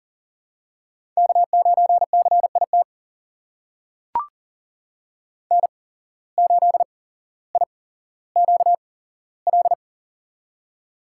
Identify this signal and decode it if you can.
Morse code, heard as a keyed tone, about 40 wpm, spaced as for 17 wpm, 700 Hz. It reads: K9CIT N8IQL
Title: USA Call Signs - 5 Characters - 8x Character Spacing - 40wpm